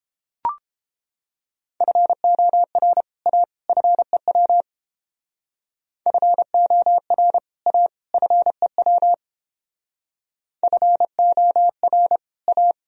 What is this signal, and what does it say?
Morse code, heard as a keyed tone, about 30 wpm, 700 Hz. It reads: FOR A FEW FOR A FEW FOR A